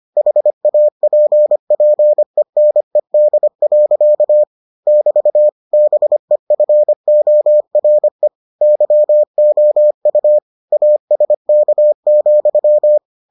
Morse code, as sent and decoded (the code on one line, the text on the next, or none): .... .- .--. .--. -. -.. .-.-.- / -...- / -... ..-. --- .-. / -.-- --- ..- / .- ... -.- --..--
HAPPENED. <BT> BEFORE YOU ASK,